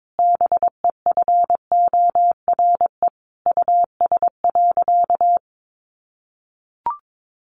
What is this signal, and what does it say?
Morse code, heard as a keyed tone, about 22 wpm, 700 Hz. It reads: BEFORE US.
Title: Sentences from Top 600 Words 22wpm